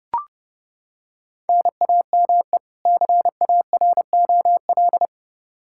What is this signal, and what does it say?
Morse code, heard as a keyed tone, about 32 wpm, 700 Hz. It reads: NAME CAROL
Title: K1USN SST QSO Practice 30wpm